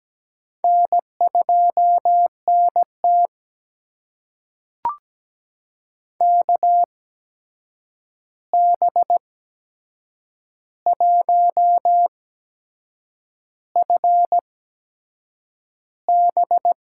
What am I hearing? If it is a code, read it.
Morse code, heard as a keyed tone, about 17 wpm, spaced as for 7 wpm, 700 Hz. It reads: N2NT KB1FB
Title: USA Call Signs - 8x Character Spacing - 17wpm